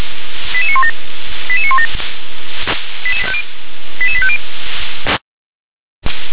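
Five-tone selective call (ZVEI), heard as quick runs of five tones, tones of about 70 ms. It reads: 88917 88917 88850
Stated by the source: Five-tone ZVEI selective call received on an SDR.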